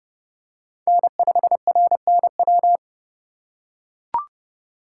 Morse code, heard as a keyed tone, about 29 wpm, 700 Hz. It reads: N5RNW